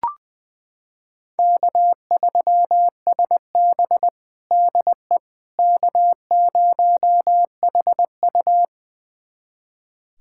Morse code, heard as a keyed tone, about 21 wpm, 700 Hz. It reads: K3SB DE K0HU
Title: USA Callsign DE Callsign - 4 Characters - 20wpm